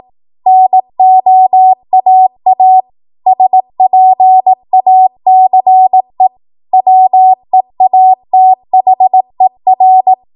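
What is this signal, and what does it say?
Morse code, heard as a keyed tone, about 18 wpm, 750 Hz. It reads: NOAA SPACE WEATHER
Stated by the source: Code practice at 18 WPM, a transition run bridging 15 to 20 WPM.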